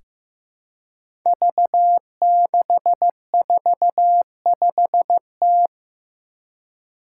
Morse code, heard as a keyed tone, about 15 wpm, 700 Hz. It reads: V645T
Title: Five Challenging Letter-Number Combinations 15wpm